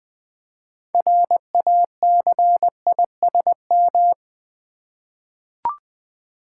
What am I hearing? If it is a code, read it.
Morse code, heard as a keyed tone, about 20 wpm, 700 Hz. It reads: RACISM
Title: 6 Letter Words - First 3 Letters Given - 20wpm